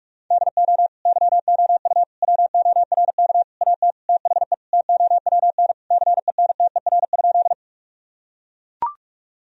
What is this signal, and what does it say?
Morse code, heard as a keyed tone, about 41 wpm, 700 Hz. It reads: DO YOU WORK AT THE TOWN CENTER?
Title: Sentences from Top 1000 Words - Review Part 2 - 45wpm